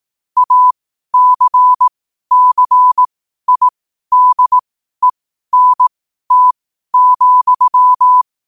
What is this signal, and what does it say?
Morse code, heard as a keyed tone, about 18 wpm, spaced as for 12 wpm, 1000 Hz. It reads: ACCIDENT,
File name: qotdfiles / 12wpm